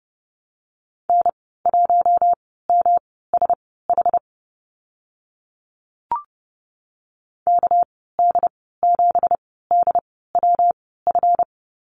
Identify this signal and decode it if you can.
Morse code, heard as a keyed tone, about 30 wpm, spaced as for 17 wpm, 700 Hz. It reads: N1MSH KD7DWF
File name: Lesson 37g - Letter x - Call Signs 30wpm